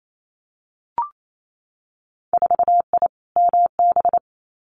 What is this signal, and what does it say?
Morse code, heard as a keyed tone, about 28 wpm, 700 Hz. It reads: 4I MB
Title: Winter Field Day Exchanges Encore 28wpm